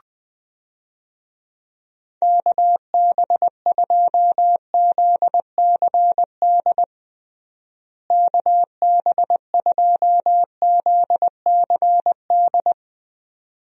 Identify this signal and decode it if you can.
Morse code, heard as a keyed tone, about 20 wpm, 700 Hz. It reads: KB2ZCD KB2ZCD
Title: USA Call Signs - 6 Characters - Repeated 2x - 20wpm